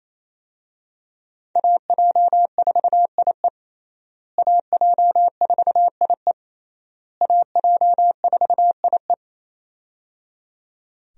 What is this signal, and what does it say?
Morse code, heard as a keyed tone, about 28 wpm, 700 Hz. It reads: AJ4IE AJ4IE AJ4IE
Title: USA Call Signs - 5 Characters - Repeated 3x - 28wpm